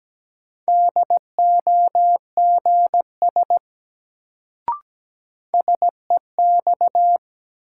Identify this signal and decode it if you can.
Morse code, heard as a keyed tone, about 17 wpm, 700 Hz. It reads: DOGS SEX
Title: Sentences from Top 1600 Words - New Word Drill Alt - 17wpm